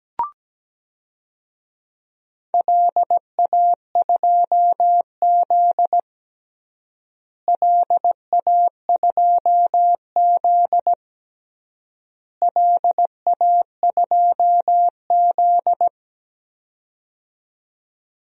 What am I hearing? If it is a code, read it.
Morse code, heard as a keyed tone, about 17 wpm, 700 Hz. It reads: LA2Z LA2Z LA2Z
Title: DXCC-100 Call Signs - 4 Characters - Repeated 3x - 17wpm